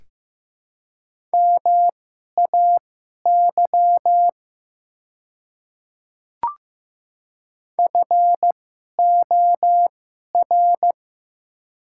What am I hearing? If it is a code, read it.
Morse code, heard as a keyed tone, about 15 wpm, spaced as for 11 wpm, 690 Hz. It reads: MAY FOR
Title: 2x Character Spacing - Top 300 Words - 15wpm